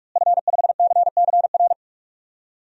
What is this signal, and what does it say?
Morse code, heard as a keyed tone, about 46 wpm, 700 Hz. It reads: F5CCR
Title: DXCC-100 Call Signs - 5 Characters - Repeated 3x - 45wpm